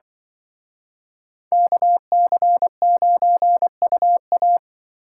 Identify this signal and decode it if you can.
Morse code, heard as a keyed tone, about 25 wpm, 700 Hz. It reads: KC9UA
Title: Speed Racing - USA 2x2 Call Signs 20wpm